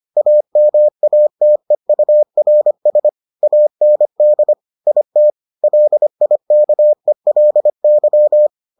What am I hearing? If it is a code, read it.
Morse code, heard as a keyed tone, about 25 wpm, 600 Hz. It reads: AMATEURS AND IT LIKELY